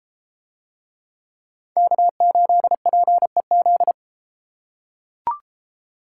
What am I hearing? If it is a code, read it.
Morse code, heard as a keyed tone, about 32 wpm, 700 Hz. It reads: K8PEZ